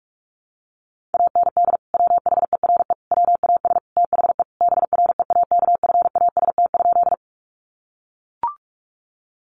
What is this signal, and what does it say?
Morse code, heard as a keyed tone, about 44 wpm, 700 Hz. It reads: AND WHERE WAS THE BREAKFAST?
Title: Sentences from Top 1500 Words 45wpm